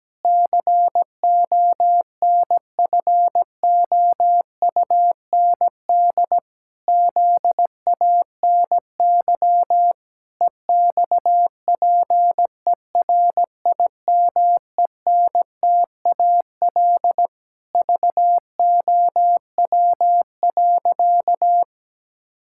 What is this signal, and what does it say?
Morse code, heard as a keyed tone, about 17 wpm, 700 Hz. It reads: CONFOUND ZANY EXPERIMENTAL VOW.
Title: Pangram Challenge v2 - 17wpm